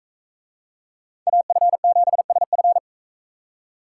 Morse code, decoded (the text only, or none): AF7SF